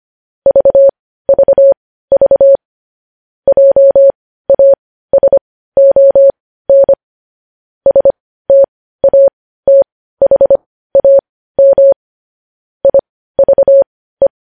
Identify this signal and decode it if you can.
Morse code, heard as a keyed tone, about 25 wpm, spaced as for 15 wpm, 550 Hz. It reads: VVV JASON STATHAM IVE
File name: podcast / News headlines-1x-25x15